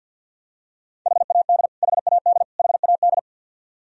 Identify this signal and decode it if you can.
Morse code, heard as a keyed tone, about 46 wpm, 700 Hz. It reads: HAD HAD HAD
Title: Top 100 Words Repeated 50wpm